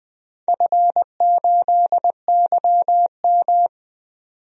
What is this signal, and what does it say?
Morse code, heard as a keyed tone, about 21 wpm, 700 Hz. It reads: F8YM
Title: DXCC-100 Call Signs - 4 Characters - 20wpm